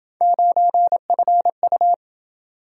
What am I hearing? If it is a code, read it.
Morse code, heard as a keyed tone, about 28 wpm, 700 Hz. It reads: 9FU